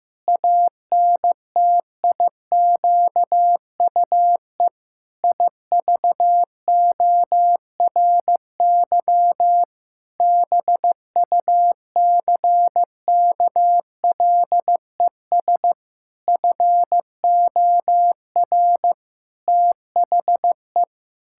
Morse code, heard as a keyed tone, about 15 wpm, 700 Hz. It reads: ANTIQUE IVORY BUCKLES FOR THE
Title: Pangram Challenge v2 - 15wpm